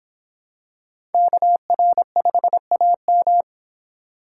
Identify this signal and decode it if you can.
Morse code, heard as a keyed tone, about 26 wpm, 700 Hz. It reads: KR5AM